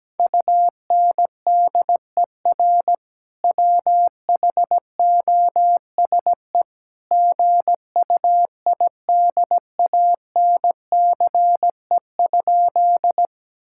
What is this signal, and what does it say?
Morse code, heard as a keyed tone, about 17 wpm, 700 Hz. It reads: UNDER WHOSE GUIDANCE?
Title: Sentences from Top 1600 Words 17wpm